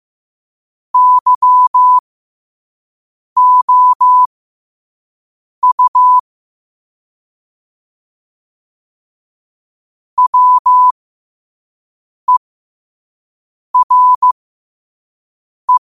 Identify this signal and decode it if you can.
Morse code, heard as a keyed tone, about 15 wpm, spaced as for 5 wpm, 1000 Hz. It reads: YOU WERE